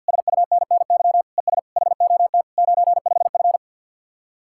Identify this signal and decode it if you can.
Morse code, heard as a keyed tone, about 64 wpm, 700 Hz. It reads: SUNNY ES HOT 95F